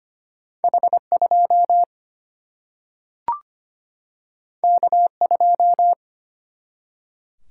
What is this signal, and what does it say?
Morse code, heard as a keyed tone, about 25 wpm, 700 Hz. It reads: H2 K2
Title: Single-Letter Single-Number 25wpm